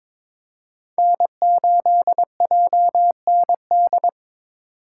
Morse code, heard as a keyed tone, about 22 wpm, 700 Hz. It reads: N8JND
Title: USA Call Signs - 5 Characters - 2x Character Spacing - 22wpm